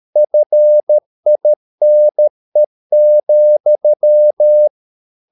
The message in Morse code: FINE,

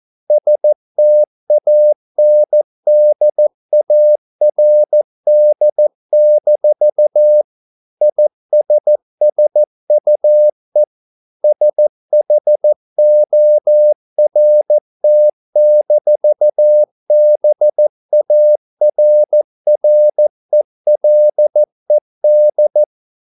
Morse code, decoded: STANDARD-ISSUE SHORT-BARRELED

